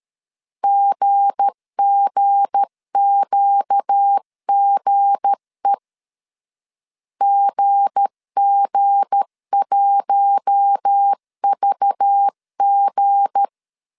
Morse code, heard as a keyed tone, about 13 wpm, 800 Hz. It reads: GGQGE GG1VG
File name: LEZIONI / 7 GF9Z / 7 G 12wpm 2055